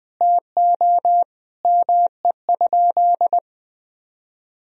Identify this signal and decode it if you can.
Morse code, heard as a keyed tone, about 20 wpm, 700 Hz. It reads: TO ME?
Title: Sentences from Top 500 Words and Abbreviations 20wpm